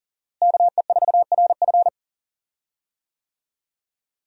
Morse code, heard as a keyed tone, about 40 wpm, 700 Hz. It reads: KE4RF
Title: USA Call Signs - 5 Characters - Repeated 3x - 40wpm